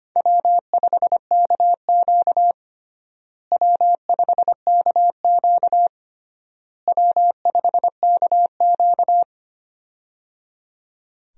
Morse code, heard as a keyed tone, about 25 wpm, 700 Hz. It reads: W5KQ W5KQ W5KQ
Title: USA Call Signs - 4 Characters - Repeated 3x - 25wpm